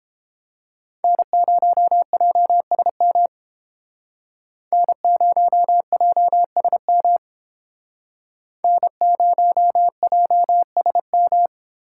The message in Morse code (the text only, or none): N0JSM N0JSM N0JSM